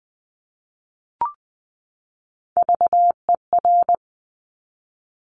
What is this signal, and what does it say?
Morse code, heard as a keyed tone, about 20 wpm, 700 Hz. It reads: VER
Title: Top Three Letter Combinations in Words 20wpm